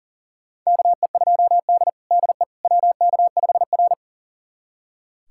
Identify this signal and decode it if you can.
Morse code, heard as a keyed tone, about 40 wpm, 700 Hz. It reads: KE2D DE WK5R
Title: USA Callsign DE Callsign - 4 Characters - 40wpm